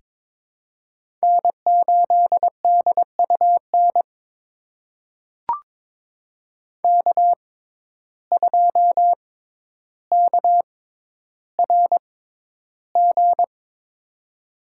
Morse code, heard as a keyed tone, about 21 wpm, spaced as for 11 wpm, 700 Hz. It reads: N8DUN K2KRG
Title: USA Call Signs - 5 Characters - 6x Character Spacing - 22wpm